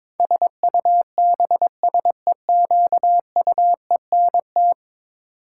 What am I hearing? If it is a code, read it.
Morse code, heard as a keyed tone, about 22 wpm, 700 Hz. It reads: SUBSEQUENT
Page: morse-code-ninja-practice-22wpm